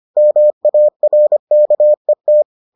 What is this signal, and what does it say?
Morse code, heard as a keyed tone, about 25 wpm, 600 Hz. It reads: MARKET